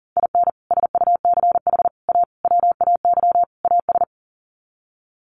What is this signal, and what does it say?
Morse code, heard as a keyed tone, about 40 wpm, 700 Hz. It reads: IN SUCH A WAY AS
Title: Sets of 5 Words Encore 40wpm